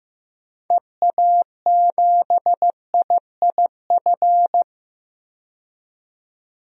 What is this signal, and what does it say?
Morse code, heard as a keyed tone, about 15 wpm, 700 Hz. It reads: EA7IIF
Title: DXCC-100 Call Signs - 6 Characters - 15wpm